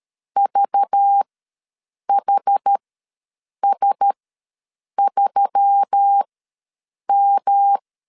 Morse code, heard as a keyed tone, about 13 wpm, spaced as for 7 wpm, 800 Hz. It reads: VHS3M
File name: LEZIONI / 7 GF9Z / 7 GF9 ws 12wpm 3344